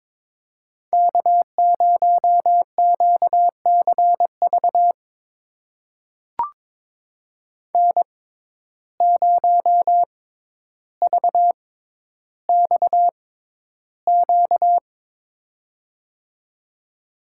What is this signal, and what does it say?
Morse code, heard as a keyed tone, about 22 wpm, spaced as for 11 wpm, 700 Hz. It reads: K0QCV N0VXQ
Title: USA Call Signs - 5 Characters - 6x Character Spacing - 22wpm